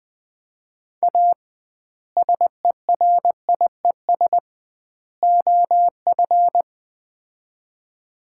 Morse code, heard as a keyed tone, about 20 wpm, 700 Hz. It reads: A SERIES OF
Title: 2x Word-Spacing - Sets of 3 Words - 20wpm